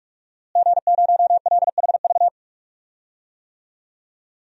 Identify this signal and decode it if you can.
Morse code, heard as a keyed tone, about 43 wpm, 700 Hz. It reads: G0LHV
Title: DXCC-100 Call Signs - 5 Characters - Repeated 3x - 45wpm